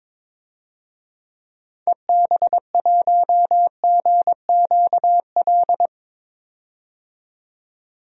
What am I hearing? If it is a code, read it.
Morse code, heard as a keyed tone, about 22 wpm, 700 Hz. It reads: EB1GQL
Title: DXCC-100 Call Signs - 6 Characters - 22wpm